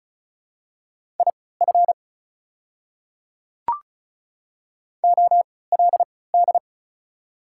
Morse code, heard as a keyed tone, about 36 wpm, spaced as for 20 wpm, 700 Hz. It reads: IF OLD
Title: 3x Character Spacing - Top 200 Words - 35wpm